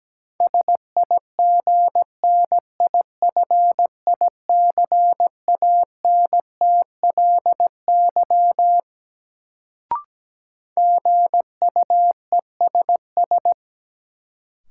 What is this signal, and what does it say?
Morse code, heard as a keyed tone, about 17 wpm, 700 Hz. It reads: SIGNIFICANTLY GUESS
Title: Sentences from Top 1500 Words - New Word Drill Alt - 17wpm